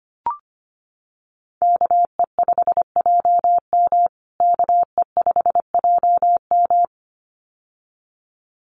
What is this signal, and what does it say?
Morse code, heard as a keyed tone, about 25 wpm, 700 Hz. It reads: KE5JM KE5JM
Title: USA Call Signs Repeated 2x 25wpm